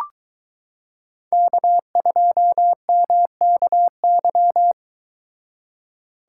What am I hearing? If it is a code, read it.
Morse code, heard as a keyed tone, about 22 wpm, 700 Hz. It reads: K2MKY